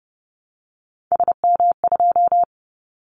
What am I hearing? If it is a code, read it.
Morse code, heard as a keyed tone, about 30 wpm, 700 Hz. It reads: SM2
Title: Callsign Prefixes for Top 25 DXCC - 30wpm